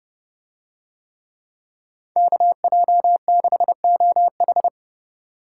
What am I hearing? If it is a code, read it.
Morse code, heard as a keyed tone, about 30 wpm, 700 Hz. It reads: KJ6OH